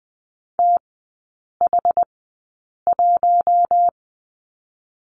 Morse code, heard as a keyed tone, about 20 wpm, spaced as for 8 wpm, 700 Hz. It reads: TH1
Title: 087 Lesson 15e - Number 5 - Three Characters 20wpm